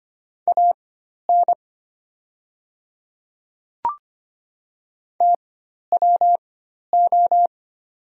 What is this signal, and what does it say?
Morse code, heard as a keyed tone, about 25 wpm, spaced as for 12 wpm, 700 Hz. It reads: AN TWO
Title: 4x Character Spacing - Top 200 Words - 25wpm